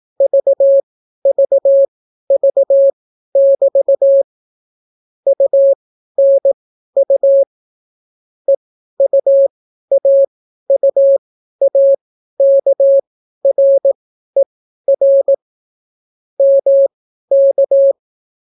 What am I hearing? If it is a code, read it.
Morse code, heard as a keyed tone, about 18 wpm, spaced as for 12 wpm, 550 Hz. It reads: VVV<BT> UNU EUAUAKRER MK